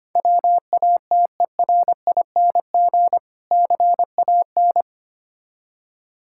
Morse code, heard as a keyed tone, about 25 wpm, 700 Hz. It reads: WATERING CAN
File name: practice / 25wpm / Difficult Words with Hints - 25wpm